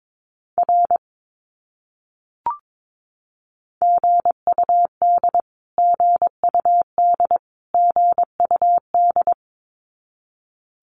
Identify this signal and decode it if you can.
Morse code, heard as a keyed tone, about 22 wpm, 700 Hz. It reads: R GUD GUD GUD
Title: Abbreviations Repeated 3x - 22wpm